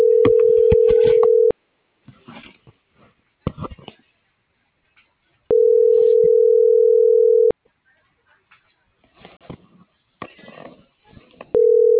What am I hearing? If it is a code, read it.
Ringback tone (North America), 440 + 480 Hz.